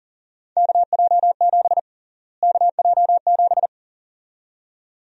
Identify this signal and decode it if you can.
Morse code, heard as a keyed tone, about 40 wpm, 700 Hz. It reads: KJ7 KJ7
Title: USA Call Sign Prefixes - Repeated 2x - 40wpm